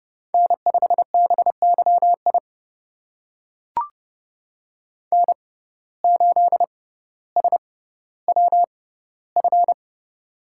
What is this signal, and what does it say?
Morse code, heard as a keyed tone, about 30 wpm, spaced as for 15 wpm, 700 Hz. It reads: N5BYI N8SWF
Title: USA Call Signs - 5 Characters - 6x Character Spacing - 30wpm